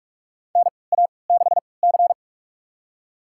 Morse code, heard as a keyed tone, about 46 wpm, spaced as for 25 wpm, 690 Hz. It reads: NA6C
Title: USA Call Signs - 4 Characters - 3x Character Spacing - 45wpm